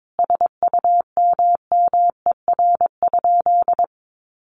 Morse code, heard as a keyed tone, about 22 wpm, 700 Hz. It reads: SUMMER?
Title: Sentences from Top 1200 Words 22wpm